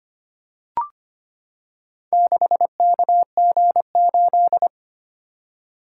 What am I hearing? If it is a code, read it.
Morse code, heard as a keyed tone, about 25 wpm, 700 Hz. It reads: 6KG8
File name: practice / 25wpm / Four Letter-Number Combinations - 25wpm